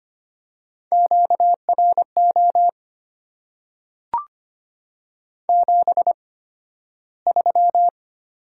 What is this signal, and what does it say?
Morse code, heard as a keyed tone, about 25 wpm, spaced as for 13 wpm, 700 Hz. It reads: QRO 73
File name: Top 90 QSO Elements - 8x Character Spacing - 25wpm